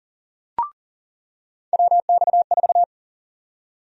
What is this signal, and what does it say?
Morse code, heard as a keyed tone, about 40 wpm, 700 Hz. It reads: WX4
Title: USA Call Sign Prefixes - 40wpm